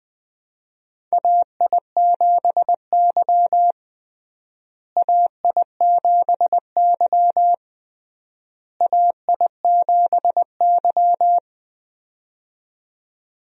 Morse code, heard as a keyed tone, about 20 wpm, 700 Hz. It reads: AI7Y AI7Y AI7Y
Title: USA Call Signs - 4 Characters - Repeated 3x - 20wpm